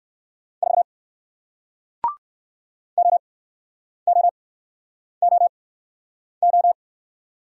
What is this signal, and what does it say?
Morse code, heard as a keyed tone, about 59 wpm, spaced as for 9 wpm, 700 Hz. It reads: 4 OOOO